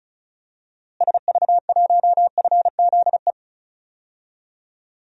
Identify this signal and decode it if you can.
Morse code, heard as a keyed tone, about 36 wpm, 700 Hz. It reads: SV1FZE
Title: DXCC-100 Call Signs - 6 Characters - Repeated 3x - 35wpm